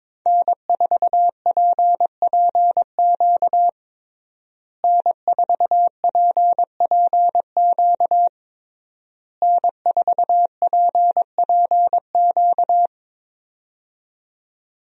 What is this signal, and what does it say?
Morse code, heard as a keyed tone, about 22 wpm, 700 Hz. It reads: N4PPQ N4PPQ N4PPQ
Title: USA Call Signs - 5 Characters - Repeated 3x - 22wpm